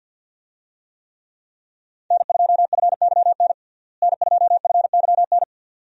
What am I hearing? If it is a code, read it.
Morse code, heard as a keyed tone, about 50 wpm, 700 Hz. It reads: N2FYN N2FYN
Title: USA Call Signs - 5 Characters - Repeated 2x - 50wpm